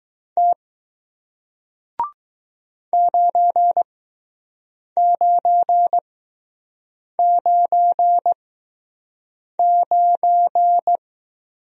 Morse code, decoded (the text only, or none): T 9999